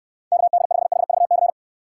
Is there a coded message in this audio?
Yes: Morse code from 0.3 to 1.5 s, heard as a keyed tone, about 69 wpm, 700 Hz. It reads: KD5HVC